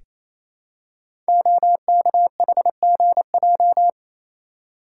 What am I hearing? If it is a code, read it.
Morse code, heard as a keyed tone, about 29 wpm, 700 Hz. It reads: OKHGJ
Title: Code Groups - 5 Letter:Numbers - 28wpm